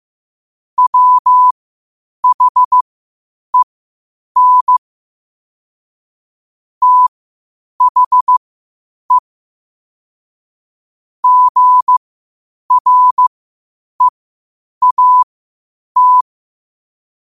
Morse code, heard as a keyed tone, about 15 wpm, spaced as for 8 wpm, 1000 Hz. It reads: WHEN THE GREAT
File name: qotdfiles / 8wpm